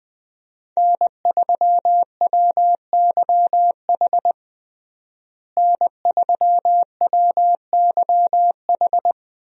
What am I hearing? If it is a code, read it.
Morse code, heard as a keyed tone, about 20 wpm, 700 Hz. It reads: N3WYH N3WYH
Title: USA Call Signs - 5 Characters - Repeated 3x - 20wpm